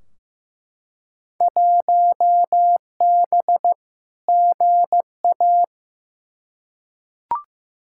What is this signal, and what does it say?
Morse code, heard as a keyed tone, about 15 wpm, 700 Hz. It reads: 1B GA
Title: Field Day Exchanges Encore - 15wpm